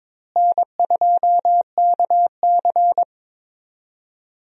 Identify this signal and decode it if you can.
Morse code, heard as a keyed tone, about 21 wpm, 700 Hz. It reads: N2KC
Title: USA Call Signs - 4 Characters - Repeated 3x - 22wpm